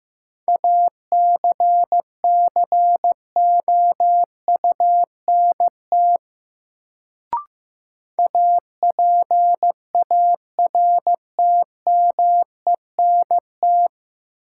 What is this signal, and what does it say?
Morse code, heard as a keyed tone, about 15 wpm, 700 Hz. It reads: ACCOUNT APARTMENT
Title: Sentences from Top 900 Words - New Word Drill - 15wpm